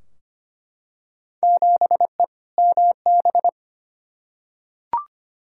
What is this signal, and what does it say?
Morse code, heard as a keyed tone, about 25 wpm, 700 Hz. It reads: 7E MB
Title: Field Day Exchanges Encore - 25wpm